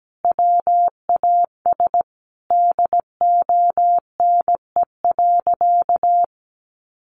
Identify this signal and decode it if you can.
Morse code, heard as a keyed tone, about 17 wpm, 700 Hz. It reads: WAS DONE.
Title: Sentences from Top 1500 Words 17wpm